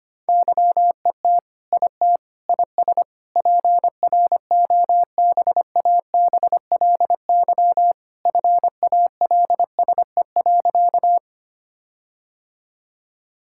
Morse code, heard as a keyed tone, about 25 wpm, 700 Hz. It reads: YET IT IS PROBABLY FALSE.
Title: Sentences from Top 1600 Words 25wpm